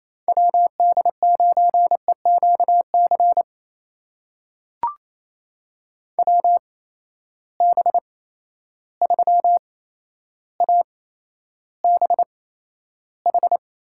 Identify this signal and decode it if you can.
Morse code, heard as a keyed tone, about 27 wpm, spaced as for 12 wpm, 700 Hz. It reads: WD9EQC WB3ABH
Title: USA Call Signs - 6 Characters - 8x Character Spacing - 28wpm